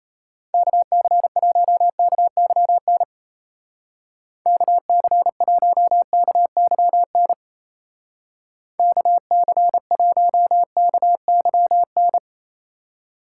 Morse code, heard as a keyed tone, about 34 wpm, 700 Hz. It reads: KC1KYN KC1KYN KC1KYN
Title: Speed Racing - USA 2x3 Call Signs 28wpm